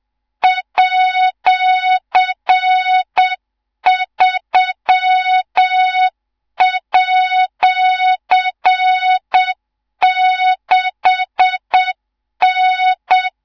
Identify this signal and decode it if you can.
Morse code, heard as a keyed tone, about 7 wpm, 3700 Hz, with noise. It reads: @3@6N
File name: Morze-HA2MN / hang / 18 IRASJELEK2